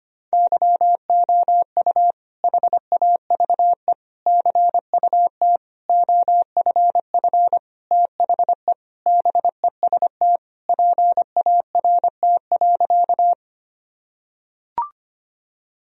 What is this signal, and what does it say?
Morse code, heard as a keyed tone, about 25 wpm, 700 Hz. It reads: YOU HAVE CUT OFF THE BEST PART.
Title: Sentences from Top 1000 Words - Review Part 1 - 25wpm